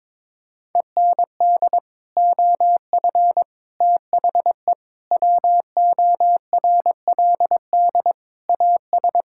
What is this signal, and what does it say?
Morse code, heard as a keyed tone, about 22 wpm, 700 Hz. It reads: END OF THE WORLD AS
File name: practice / 22wpm / Sets of 8 Words - 22wpm